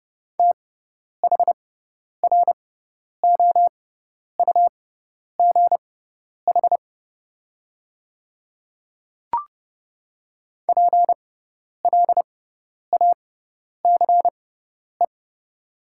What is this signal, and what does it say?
Morse code, heard as a keyed tone, about 30 wpm, spaced as for 10 wpm, 700 Hz. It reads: THROUGH PLACE
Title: 6x Character Spacing - Top 200 Words - 30wpm